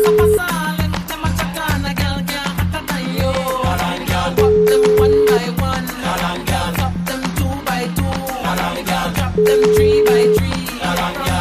Ringback tone (Europe, 425 Hz), with noise.